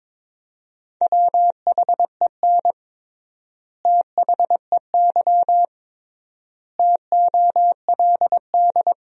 Morse code, heard as a keyed tone, about 22 wpm, 700 Hz. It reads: WHEN THEY TOLD
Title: 3x Word-Spacing - Sentences from Top 400 Words - 22wpm